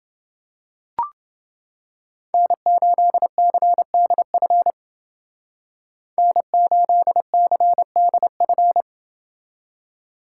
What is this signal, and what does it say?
Morse code, heard as a keyed tone, about 29 wpm, 700 Hz. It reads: N8CDF N8CDF